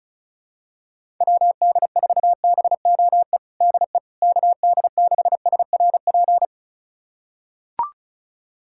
Morse code, heard as a keyed tone, about 36 wpm, 700 Hz. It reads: WD4BOE DE KD6SRP
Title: USA Callsign DE Callsign 35wpm